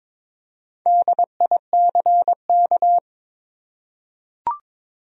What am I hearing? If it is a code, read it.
Morse code, heard as a keyed tone, about 21 wpm, 700 Hz. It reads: DICK
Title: USA Common Names - 22wpm